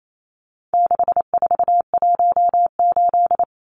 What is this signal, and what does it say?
Morse code, heard as a keyed tone, about 28 wpm, 700 Hz. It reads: 6418
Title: Numbers - 4-Digit 28wpm